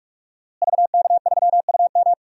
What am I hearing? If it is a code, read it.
Morse code, heard as a keyed tone, about 44 wpm, 700 Hz. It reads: VK3UM
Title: DXCC-100 Call Signs - 5 Characters - Repeated 3x - 45wpm